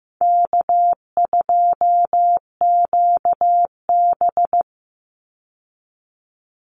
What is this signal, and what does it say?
Morse code, heard as a keyed tone, about 15 wpm, 700 Hz. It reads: K2QB
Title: Easy Call Signs Repeated 2x - 15wpm